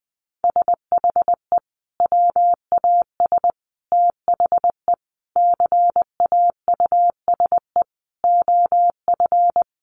Morse code, heard as a keyed tone, about 20 wpm, 700 Hz. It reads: SHE WAS THE CAUSE OF
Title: Sentences from Top 200 Words 20wpm